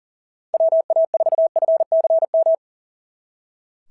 Morse code, heard as a keyed tone, about 37 wpm, 660 Hz. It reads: WA4FCM
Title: SOTA QSO Practice 40wpm